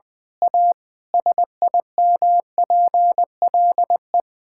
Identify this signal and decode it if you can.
Morse code, heard as a keyed tone, about 21 wpm, 700 Hz. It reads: A SIMPLE
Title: Sentences from Top 1000 Words - Review Part 6 - 20wpm